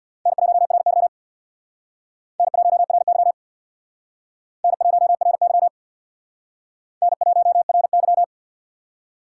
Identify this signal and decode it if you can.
Morse code, heard as a keyed tone, about 66 wpm, 700 Hz. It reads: N1RY N1RY N1RY N1RY